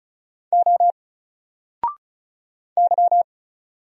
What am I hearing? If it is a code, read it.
Morse code, heard as a keyed tone, about 34 wpm, 700 Hz. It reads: O Y
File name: Single-Letter-Number-35wpm